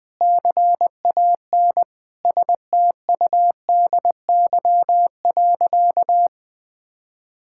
Morse code, heard as a keyed tone, about 20 wpm, 700 Hz. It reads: CAN STUDY.
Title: Sentences from Top 1000 Words - Review Part 5 - 20wpm